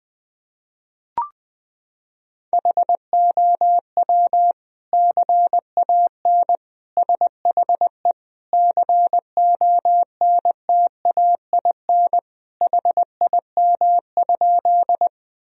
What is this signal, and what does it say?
Morse code, heard as a keyed tone, about 20 wpm, 700 Hz. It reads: HOW CAN SHE CONTAIN HIM?
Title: Sentences from Top 1000 Words - Review Part 2 - 20wpm